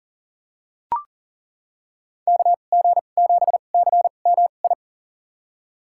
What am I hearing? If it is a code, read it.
Morse code, heard as a keyed tone, about 40 wpm, spaced as for 29 wpm, 700 Hz. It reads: KG7CMI
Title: USA Call Signs - 6 Characters - 2x Character Spacing - 40wpm